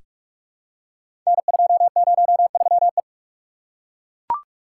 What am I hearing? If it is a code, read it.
Morse code, heard as a keyed tone, about 42 wpm, 700 Hz. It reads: N203E
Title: Code Groups - 5 Letter:Numbers - 45wpm